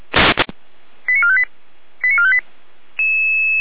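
Five-tone selective call (ZVEI), heard as quick runs of five tones, tones of about 70 ms. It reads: 89458 89458
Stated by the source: Received on an SDR, logged by Feuerwehr Orlamünde (fire brigade).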